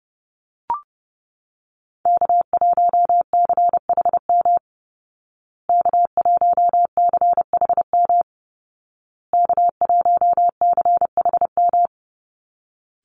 Morse code, heard as a keyed tone, about 30 wpm, 700 Hz. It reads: K1CHM K1CHM K1CHM
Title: Call Signs Repeated 2x - 30wpm